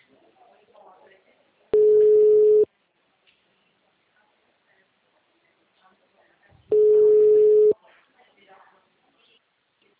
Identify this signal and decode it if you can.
Ringback tone (Europe, 425 Hz), with speech.
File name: out- -2006- - - .55382